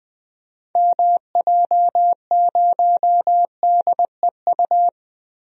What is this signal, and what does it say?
Morse code, heard as a keyed tone, about 21 wpm, 700 Hz. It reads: MJ0DEU
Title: DXCC-100 Call Signs - 6 Characters - Repeated 2x - 20wpm